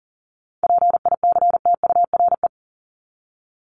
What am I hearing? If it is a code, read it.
Morse code, heard as a keyed tone, about 40 wpm, 700 Hz. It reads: PICTURE
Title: Top 200 Words 40wpm